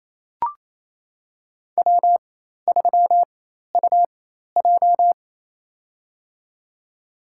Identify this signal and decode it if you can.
Morse code, heard as a keyed tone, about 27 wpm, spaced as for 13 wpm, 700 Hz. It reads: W3UJ
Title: USA Call Signs - 4 Characters - 4x Character Spacing - 28wpm